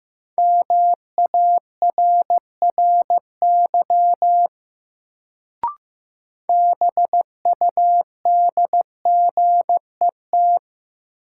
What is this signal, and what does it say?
Morse code, heard as a keyed tone, about 15 wpm, 700 Hz. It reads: MARRY BUDGET
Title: Sentences from Top 1200 Words - New Word Drill Alt - 15wpm